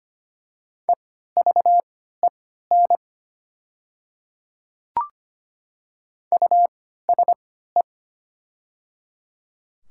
Morse code, heard as a keyed tone, about 25 wpm, spaced as for 14 wpm, 700 Hz. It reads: EVEN USE
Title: 3x Character Spacing - Top 500 Words - 25wpm